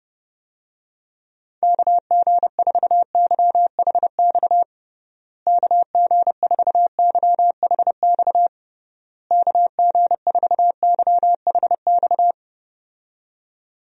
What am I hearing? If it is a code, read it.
Morse code, heard as a keyed tone, about 30 wpm, 700 Hz. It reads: KG4YHX KG4YHX KG4YHX
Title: USA Call Signs - 6 Characters - Repeated 3x - 30wpm